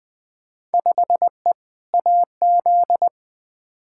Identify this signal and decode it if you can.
Morse code, heard as a keyed tone, about 20 wpm, 700 Hz. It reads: 5E AZ
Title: Field Day Exchanges - 20wpm